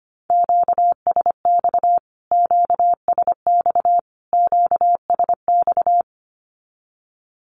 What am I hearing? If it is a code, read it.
Morse code, heard as a keyed tone, about 25 wpm, 700 Hz. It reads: QSX QSX QSX
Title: Q-Codes Repeated 3x 25wpm